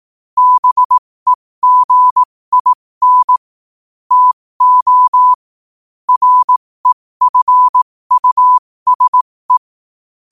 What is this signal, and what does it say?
Morse code, heard as a keyed tone, about 18 wpm, spaced as for 15 wpm, 1000 Hz. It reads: BEGIN TO REFUSE
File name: qotdfiles / 15wpm